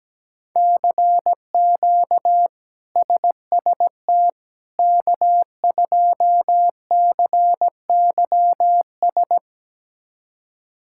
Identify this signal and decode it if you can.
Morse code, heard as a keyed tone, about 17 wpm, 700 Hz. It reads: CQ SST K2CYS
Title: K1USN SST QSO Practice - 17wpm